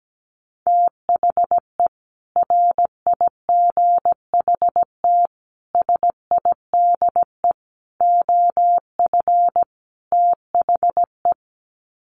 Morse code, heard as a keyed tone, about 17 wpm, 700 Hz. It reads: THE RIGHT SIDE OF THE
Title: Sets of 5 Words Encore 17wpm